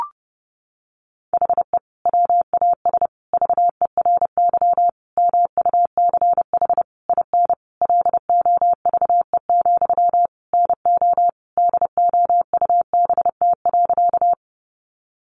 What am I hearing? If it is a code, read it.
Morse code, heard as a keyed tone, about 30 wpm, 700 Hz. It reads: HE WAS VERY MUCH IN LOVE, NO DOUBT.
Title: Sentences from Top 1500 Words 30wpm